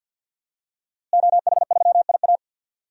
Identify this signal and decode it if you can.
Morse code, heard as a keyed tone, about 51 wpm, 700 Hz. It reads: OH3IA
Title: DXCC-100 Call Signs - 5 Characters - Repeated 3x - 50wpm